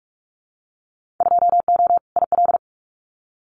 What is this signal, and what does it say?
Morse code, heard as a keyed tone, about 46 wpm, 700 Hz. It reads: 2O IL
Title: Winter Field Day Exchanges 45wpm